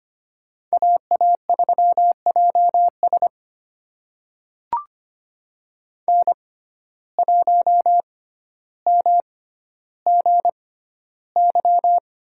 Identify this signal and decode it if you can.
Morse code, heard as a keyed tone, about 25 wpm, spaced as for 13 wpm, 700 Hz. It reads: AA3JS N1MGY
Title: USA Call Signs - 6x Character Spacing - 25wpm